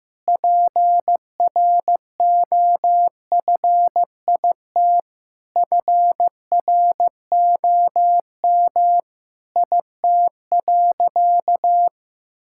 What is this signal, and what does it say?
Morse code, heard as a keyed tone, about 15 wpm, 700 Hz. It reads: PROFIT FROM IT.
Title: Sentences from Top 1000 Words - Review Part 5 - 15wpm